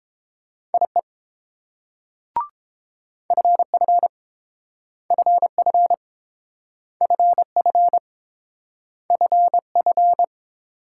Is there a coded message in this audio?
Yes: Morse code from 0.7 to 10.2 s, heard as a keyed tone, about 29 wpm, 700 Hz. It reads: IE FF FF FF FF